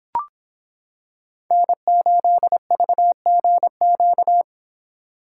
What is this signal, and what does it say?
Morse code, heard as a keyed tone, about 26 wpm, 700 Hz. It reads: N8VGQ